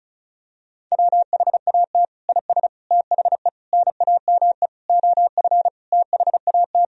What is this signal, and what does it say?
Morse code, heard as a keyed tone, about 36 wpm, 700 Hz. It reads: WHAT IS THE NAME OF THAT
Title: Sentences from Top 1000 Words - Review Part 1 - 35wpm